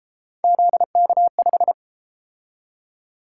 Morse code, heard as a keyed tone, about 32 wpm, 700 Hz. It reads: ZK5